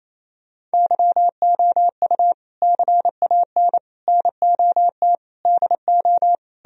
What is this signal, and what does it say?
Morse code, heard as a keyed tone, about 36 wpm, 700 Hz. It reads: YOU CAN NOT DO